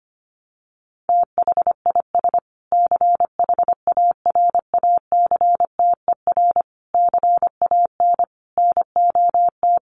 Morse code, heard as a keyed tone, about 25 wpm, 700 Hz. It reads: THIS CHARACTER CAN NOT
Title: Sets of 5 Words Encore2 25wpm